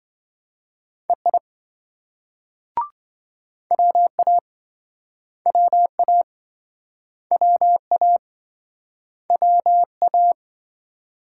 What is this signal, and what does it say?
Morse code, heard as a keyed tone, about 26 wpm, 700 Hz. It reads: EI WA WA WA WA